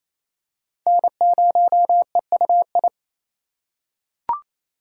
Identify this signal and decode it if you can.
Morse code, heard as a keyed tone, about 28 wpm, 700 Hz. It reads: N0EUI
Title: USA Call Signs - 3x Character Spacing - 28wpm